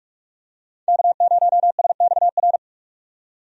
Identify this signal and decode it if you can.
Morse code, heard as a keyed tone, about 46 wpm, 700 Hz. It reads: K0SXR
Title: USA Call Signs - 5 Characters - 45wpm